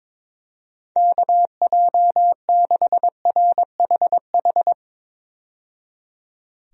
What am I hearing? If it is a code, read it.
Morse code, heard as a keyed tone, about 22 wpm, 700 Hz. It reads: KJ6RHH
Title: USA Call Signs - 6 Characters - 22wpm